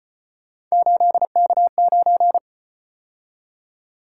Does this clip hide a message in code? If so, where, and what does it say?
Yes: Morse code from 0.7 to 2.4 s, heard as a keyed tone, about 34 wpm, 700 Hz. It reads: 8K9